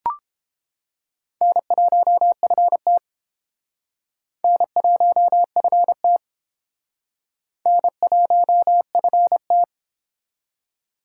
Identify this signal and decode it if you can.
Morse code, heard as a keyed tone, about 31 wpm, 700 Hz. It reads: N1FT N1FT N1FT